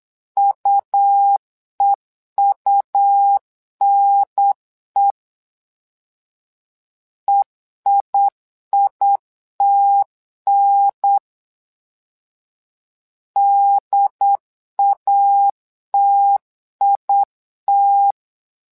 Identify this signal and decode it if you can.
Morse code, heard as a keyed tone, about 8 wpm, 800 Hz. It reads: UEUNE EIITN DATIT
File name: LEZIONI / 2 UHDN / 2 UHDN 8wpm2232